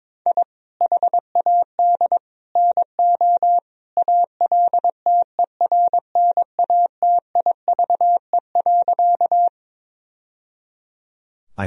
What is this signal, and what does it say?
Morse code, heard as a keyed tone, about 22 wpm, 700 Hz. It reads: I HAD NO ALTERNATIVE.
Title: Sentences from Top 1000 Words - Review Part 4 - 22wpm